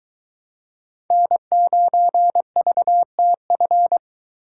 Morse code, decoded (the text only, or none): N9VTF